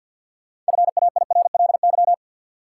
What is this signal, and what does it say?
Morse code, heard as a keyed tone, about 46 wpm, 700 Hz. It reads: FAIRLY
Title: Sentences from Top 1100 Words - New Word Drill Alt - 50wpm